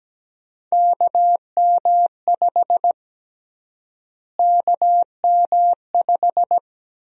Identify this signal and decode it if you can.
Morse code, heard as a keyed tone, about 17 wpm, 700 Hz. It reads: KM5 KM5
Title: USA Call Sign Prefixes - Repeated 2x - 17wpm